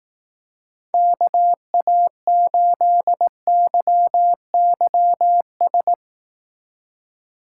Morse code, heard as a keyed tone, about 18 wpm, 700 Hz. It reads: KA8YYS
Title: Speed Racing - USA Call Signs 15wpm